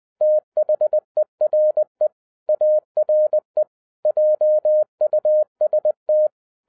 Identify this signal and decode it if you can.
Morse code, heard as a keyed tone, about 20 wpm, 610 Hz. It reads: THERE ARE JUST
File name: 2012-01-mp3 / n16-THROUGH EIGHT MONTHS-20wpm